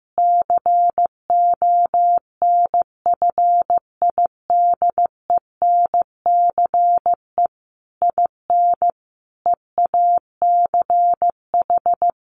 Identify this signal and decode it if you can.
Morse code, heard as a keyed tone, about 15 wpm, 700 Hz. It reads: CONFIDENCE IN EACH
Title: Sentences from Top 1500 Words 15wpm